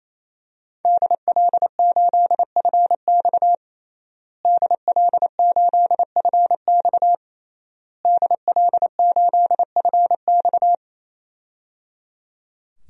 Morse code, heard as a keyed tone, about 27 wpm, 700 Hz. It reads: DL8FX DL8FX DL8FX
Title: DXCC-100 Call Signs - 5 Characters - Repeated 3x - 28wpm